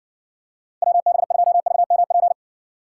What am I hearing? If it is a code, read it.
Morse code, heard as a keyed tone, about 61 wpm, 700 Hz. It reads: WB2VMP